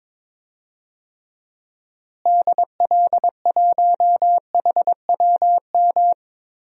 Morse code, heard as a keyed tone, about 21 wpm, 700 Hz. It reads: DL1HWM
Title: DXCC-100 Call Signs - 6 Characters - Repeated 3x - 22wpm